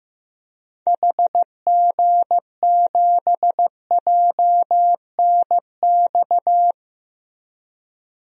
Morse code, heard as a keyed tone, about 15 wpm, 700 Hz. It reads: HG7JNX
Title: DXCC-100 Call Signs - 6 Characters - Repeated 3x - 15wpm